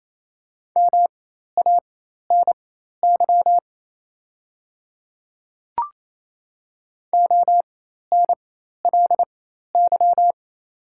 Morse code, heard as a keyed tone, about 28 wpm, spaced as for 13 wpm, 700 Hz. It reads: MANY ONLY
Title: 4x Character Spacing - Top 500 Words - 28wpm